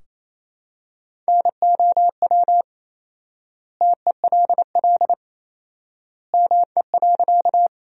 Morse code, heard as a keyed tone, about 27 wpm, 700 Hz. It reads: NOW TELL ME.
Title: 4x Word-Spacing - Sentences from Top 200 Words - 28wpm